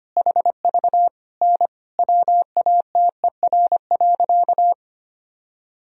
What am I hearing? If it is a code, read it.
Morse code, heard as a keyed tone, about 25 wpm, 700 Hz. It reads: HV N WATER.
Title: Sentences from Top 100 Words and Abbreviations 25wpm